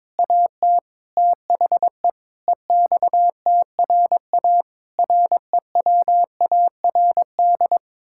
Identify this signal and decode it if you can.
Morse code, heard as a keyed tone, about 22 wpm, 700 Hz. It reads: AT THE EXTRA REWARD
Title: Pangram Challenge v2 - 22wpm